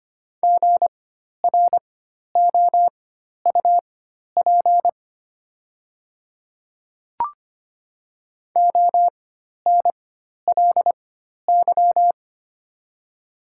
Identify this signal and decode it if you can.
Morse code, heard as a keyed tone, about 25 wpm, spaced as for 12 wpm, 700 Hz. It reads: GROUP ONLY
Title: 4x Character Spacing - Top 500 Words - 25wpm